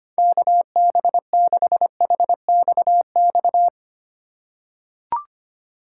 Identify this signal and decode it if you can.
Morse code, heard as a keyed tone, about 25 wpm, 700 Hz. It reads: KB6HXX
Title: USA Call Signs - 6 Characters - 2x Character Spacing - 25wpm